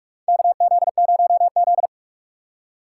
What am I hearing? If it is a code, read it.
Morse code, heard as a keyed tone, about 46 wpm, 690 Hz. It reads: KZ0Z
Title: USA Call Signs - 4 Characters - 3x Character Spacing - 45wpm